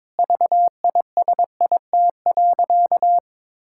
Morse code, heard as a keyed tone, about 22 wpm, 700 Hz. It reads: VISIT.
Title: Sentences from Top 1000 Words - Review Part 6 - 22wpm